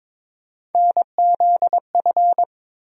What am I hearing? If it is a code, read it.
Morse code, heard as a keyed tone, about 22 wpm, 700 Hz. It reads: NZF
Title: USA Call Sign Postfixes - 3 Letters 22wpm